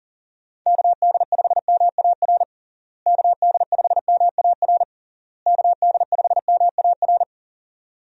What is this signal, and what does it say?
Morse code, heard as a keyed tone, about 40 wpm, 700 Hz. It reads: KD5MAR KD5MAR KD5MAR
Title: USA Call Signs - 6 Characters - Repeated 3x - 40wpm